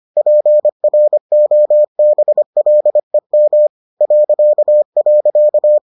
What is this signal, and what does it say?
Morse code, heard as a keyed tone, about 25 wpm, 600 Hz. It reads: PROBLEM ..